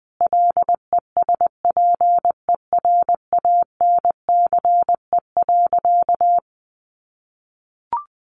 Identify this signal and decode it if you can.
Morse code, heard as a keyed tone, about 20 wpm, 700 Hz. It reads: LESPERANCE.
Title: A Sound Of Thunder 20wpm